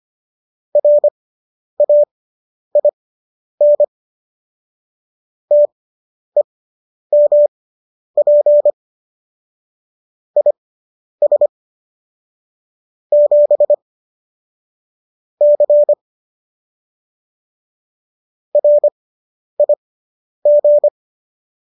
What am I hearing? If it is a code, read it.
Morse code, heard as a keyed tone, about 25 wpm, spaced as for 10 wpm, 590 Hz. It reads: RAIN TEMP IS 7 C RIG